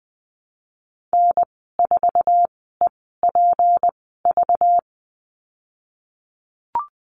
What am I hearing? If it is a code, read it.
Morse code, heard as a keyed tone, about 20 wpm, spaced as for 15 wpm, 700 Hz. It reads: N4EPV
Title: 207 Lesson 33g - Letter j - Call Signs 20wpm